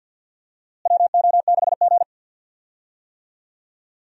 Morse code, heard as a keyed tone, about 52 wpm, 700 Hz. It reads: WO6G
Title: USA Call Signs - 4 Characters - Repeated 3x - 50wpm